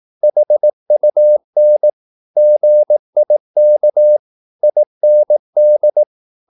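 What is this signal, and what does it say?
Morse code, heard as a keyed tone, about 18 wpm, 600 Hz. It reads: HUN GIK IND
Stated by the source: Slow (QRS) Morse practice at 18 wpm, CW med Gnister.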